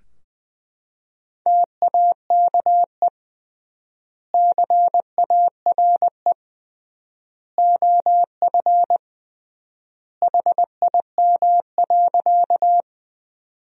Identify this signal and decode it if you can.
Morse code, heard as a keyed tone, about 20 wpm, 700 Hz. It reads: TAKE CARE OF HIM.
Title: 3x Word-Spacing - Sentences from Top 300 Words - 20wpm